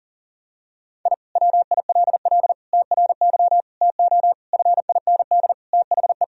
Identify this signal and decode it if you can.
Morse code, heard as a keyed tone, about 40 wpm, 700 Hz. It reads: I WILL TRY TO FIND THE
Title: Sentences from Top 1000 Words - Review Part 4 - 40wpm